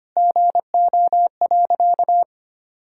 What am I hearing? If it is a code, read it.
Morse code, heard as a keyed tone, about 25 wpm, 700 Hz. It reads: GO.
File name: practice / 25wpm / Sentences with 4-Word Phrases - 25wpm